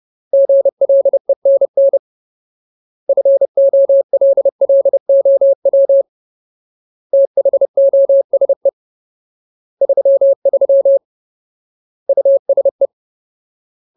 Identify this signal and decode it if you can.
Morse code, heard as a keyed tone, about 30 wpm, 550 Hz. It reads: GLENN FOLLOW THOSE 33 USE
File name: Ultimate Word List / 30WPM Ultimate